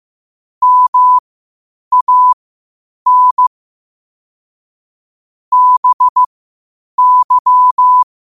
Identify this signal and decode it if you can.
Morse code, heard as a keyed tone, about 15 wpm, spaced as for 8 wpm, 1000 Hz. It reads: MAN BY
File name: qotdfiles / 8wpm